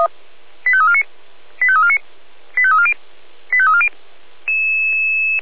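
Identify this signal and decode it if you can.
Five-tone selective call (ZVEI), heard as quick runs of five tones, tones of about 70 ms. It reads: 85349 85349 85340 85340
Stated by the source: ZVEI five-tone selective call 85276 received with an SDR, hosted by Feuerwehr Orlamünde (fire brigade).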